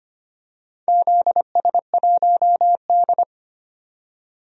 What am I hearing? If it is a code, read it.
Morse code, heard as a keyed tone, about 25 wpm, 700 Hz. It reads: ZS1D
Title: DXCC-100 Call Signs - 4 Characters - 25wpm